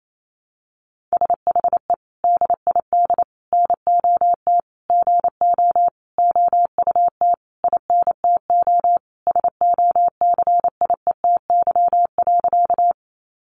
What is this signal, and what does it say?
Morse code, heard as a keyed tone, about 28 wpm, 700 Hz. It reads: SHE DID NOT GO OUT INTO SOCIETY.
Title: Sentences from Top 700 Words 28wpm